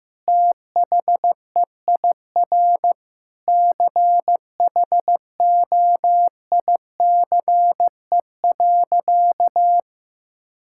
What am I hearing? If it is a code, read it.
Morse code, heard as a keyed tone, about 16 wpm, 700 Hz. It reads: THEIR CHOICE.
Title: Sentences from Top 1000 Words - Review Part 6 - 15wpm